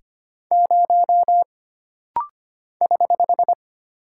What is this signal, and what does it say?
Morse code, heard as a keyed tone, about 25 wpm, 700 Hz. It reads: 0 <HH>
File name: Single Character (Letter:Number:Punctuation:Prosign) - Rapid-Fire - 25wpm